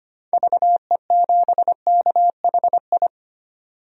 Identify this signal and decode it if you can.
Morse code, heard as a keyed tone, about 25 wpm, 700 Hz. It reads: VE7KHI
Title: DXCC-100 Call Signs - 6 Characters - 25wpm